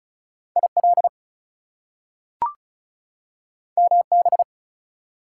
Morse code, heard as a keyed tone, about 36 wpm, 700 Hz. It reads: IL MB
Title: US States and Canadian Provinces 35wpm